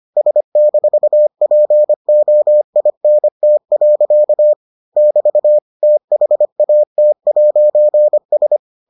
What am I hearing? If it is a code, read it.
Morse code, heard as a keyed tone, about 25 wpm, 600 Hz. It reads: S-POINT. <BT> THAT'S